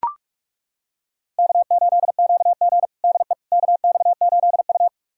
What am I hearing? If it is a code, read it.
Morse code, heard as a keyed tone, about 42 wpm, 700 Hz. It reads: K8QG DE KX8U
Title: USA Callsign DE Callsign - 4 Characters - 45wpm